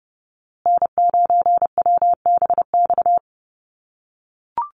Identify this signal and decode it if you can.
Morse code, heard as a keyed tone, about 30 wpm, 700 Hz. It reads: N9WBX
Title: USA Call Signs Repeated 2x 30wpm